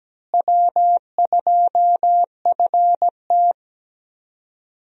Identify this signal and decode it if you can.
Morse code, heard as a keyed tone, about 17 wpm, 700 Hz. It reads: W2FT
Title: USA Call Signs - 4 Characters - Repeated 3x - 17wpm